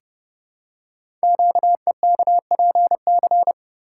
Morse code, heard as a keyed tone, about 30 wpm, 700 Hz. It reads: QEKPC
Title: Code Groups - 5 Letters - 30wpm